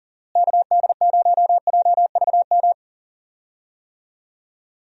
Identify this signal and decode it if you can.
Morse code, heard as a keyed tone, about 40 wpm, 700 Hz. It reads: KD0JVM
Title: USA Call Signs - 6 Characters - Repeated 2x - 40wpm